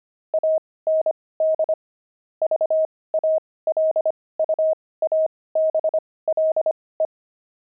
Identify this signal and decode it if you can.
Morse code, heard as a keyed tone, about 25 wpm, spaced as for 18 wpm, 630 Hz. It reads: AND VALUABLE
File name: pangram-fs-18-cs-25-grandpa